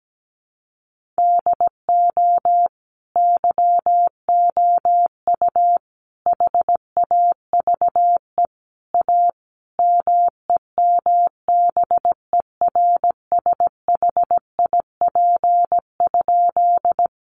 Morse code, heard as a keyed tone, about 17 wpm, 700 Hz. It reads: DO YOU HAVE A MEMBERSHIP?
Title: Sentences from Top 1400 Words 17wpm